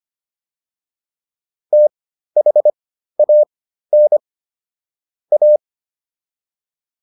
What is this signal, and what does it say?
Morse code, heard as a keyed tone, about 25 wpm, spaced as for 13 wpm, 600 Hz. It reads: THAN A